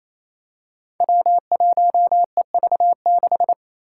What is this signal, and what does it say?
Morse code, heard as a keyed tone, about 28 wpm, 700 Hz. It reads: W1EV6
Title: Five Challenging Letter-Number Combinations 28wpm